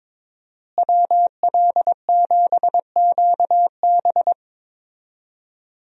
Morse code, heard as a keyed tone, about 22 wpm, 700 Hz. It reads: WL7QB
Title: USA Call Signs - 5 Characters - Repeated 2x - 22wpm